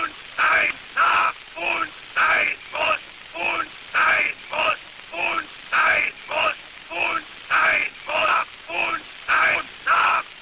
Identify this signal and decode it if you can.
Emergency Alert System (SAME) tones (header bursts), heard as harsh buzzing bursts, with speech. It is heard from 0.3 to 9.6 s.